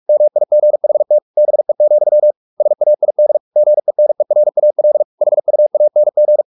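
Morse code, heard as a keyed tone, about 44 wpm, 600 Hz. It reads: MIGHT BE, SAID GENERAL HUANG
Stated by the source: Morse practice at 45 wpm, the novel Never by Ken Follett in English, from CW med Gnister.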